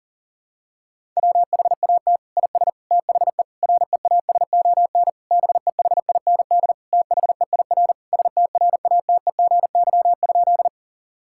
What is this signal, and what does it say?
Morse code, heard as a keyed tone, about 40 wpm, 700 Hz. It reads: WHAT IS THE REASON BEHIND THEIR STRATEGY?
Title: Sentences from Top 1000 Words - Review Part 5 - 40wpm